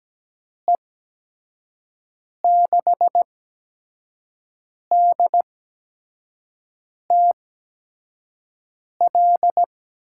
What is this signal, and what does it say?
Morse code, heard as a keyed tone, about 17 wpm, spaced as for 5 wpm, 700 Hz. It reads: E6DTL
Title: USA Call Signs - 6 Characters - 8x Character Spacing - 17wpm